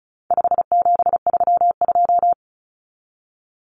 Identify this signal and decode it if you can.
Morse code, heard as a keyed tone, about 34 wpm, 700 Hz. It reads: 5732
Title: Numbers - 4-Digit 35wpm